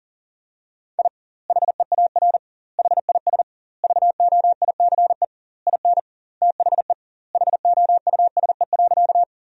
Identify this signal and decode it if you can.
Morse code, heard as a keyed tone, about 40 wpm, 700 Hz. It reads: I HEAR HIS VOICE IN THE HOUSE.
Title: 2x Word-Spacing - Sentences from Top 500 Words - 40wpm